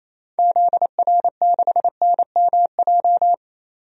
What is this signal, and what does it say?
Morse code, heard as a keyed tone, about 28 wpm, 700 Hz. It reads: ZR6NMJ
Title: DXCC-100 Call Signs - 6 Characters - 28wpm